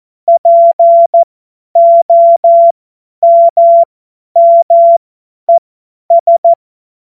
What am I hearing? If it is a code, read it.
Morse code, heard as a keyed tone, about 14 wpm, spaced as for 10 wpm, 680 Hz. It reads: POMMES
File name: Morsekurs / Witze / Witz 13. Januar 2026 50 F70